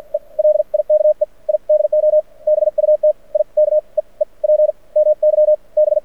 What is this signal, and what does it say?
Morse code, heard as a keyed tone, about 57 wpm, spaced as for 41 wpm, 600 Hz. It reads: E LIKE I DO BUT I KEEP MY B
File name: archived / CRSnet